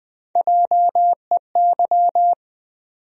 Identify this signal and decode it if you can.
Morse code, heard as a keyed tone, about 20 wpm, 700 Hz. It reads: JEY
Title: USA Call Sign Postfixes - 3 Letters 20wpm